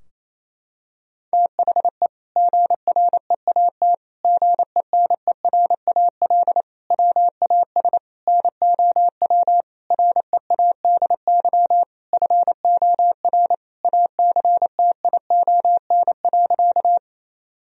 Morse code, heard as a keyed tone, about 28 wpm, 700 Hz. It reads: THE GREAT GENERAL WAS NOW READY FOR ACTION.
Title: Sentences from Top 1000 Words - Review Part 4 - 28wpm